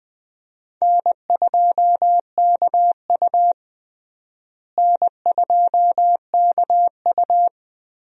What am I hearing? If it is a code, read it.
Morse code, heard as a keyed tone, about 20 wpm, 700 Hz. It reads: N2KU N2KU
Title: USA Call Signs - 4 Characters - Repeated 2x - 20wpm